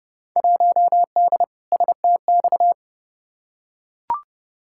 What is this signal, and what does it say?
Morse code, heard as a keyed tone, about 30 wpm, 700 Hz. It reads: 1D STX
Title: Field Day Exchanges - 30wpm